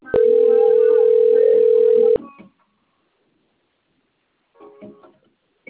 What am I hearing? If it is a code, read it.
Ringback tone (North America), 440 + 480 Hz.